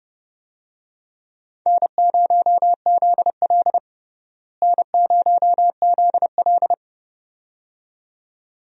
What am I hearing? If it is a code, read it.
Morse code, heard as a keyed tone, about 32 wpm, 700 Hz. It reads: N0ZL N0ZL
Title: USA Call Signs - 4 Characters - Repeated 2x - 30wpm